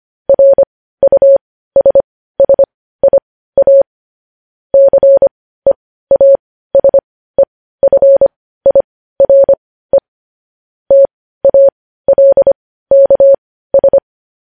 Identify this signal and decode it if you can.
Morse code, heard as a keyed tone, about 25 wpm, spaced as for 15 wpm, 550 Hz. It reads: RUSSIA CEASEFIRE TALKS
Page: news-headlines-in-morse-code-at-15-wpm